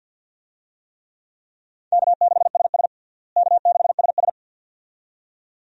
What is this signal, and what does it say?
Morse code, heard as a keyed tone, about 52 wpm, 700 Hz. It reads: K6SS K6SS
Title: USA Call Signs - 4 Characters - Repeated 2x - 50wpm